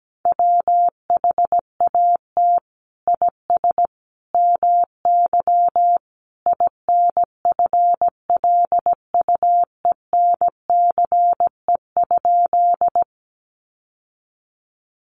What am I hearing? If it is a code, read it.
Morse code, heard as a keyed tone, about 17 wpm, 700 Hz. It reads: WHAT IS MY INFLUENCE?
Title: Sentences from Top 1100 Words 17wpm